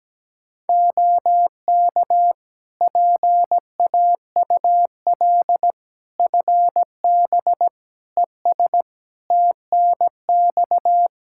Morse code, heard as a keyed tone, about 17 wpm, 700 Hz. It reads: OK PAUL FB ES TNX
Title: K1USN SST QSO Practice 17wpm